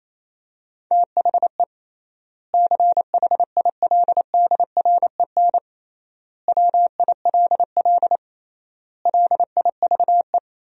Morse code, heard as a keyed tone, about 28 wpm, 700 Hz. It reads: THE CHILDREN WILL LIVE
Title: 3x Word-Spacing - Sentences from Top 300 Words - 28wpm